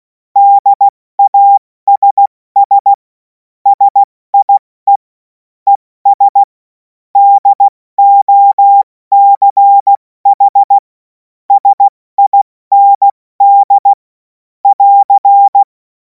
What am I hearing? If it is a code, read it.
Morse code, heard as a keyed tone, about 16 wpm, spaced as for 14 wpm, 800 Hz. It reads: DASS SIE ES DOCH SIND <AR>
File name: Morsekurs / Witze / Witz 07. April 2026 70 F80